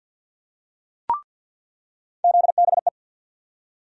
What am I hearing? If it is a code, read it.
Morse code, heard as a keyed tone, about 50 wpm, 700 Hz. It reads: ZBE
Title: USA Call Sign Postfixes - 3 Letters 50wpm